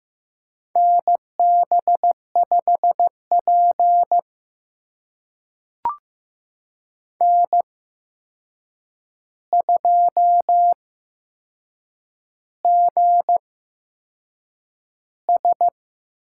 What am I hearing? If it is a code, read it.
Morse code, heard as a keyed tone, about 15 wpm, spaced as for 6 wpm, 700 Hz. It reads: NB5P N2GS
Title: USA Call Signs - 4 Characters - 8x Character Spacing - 15wpm